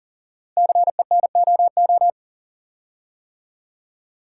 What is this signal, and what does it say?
Morse code, heard as a keyed tone, about 42 wpm, 700 Hz. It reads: CENOO
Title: Code Groups - 5 Letters - 40wpm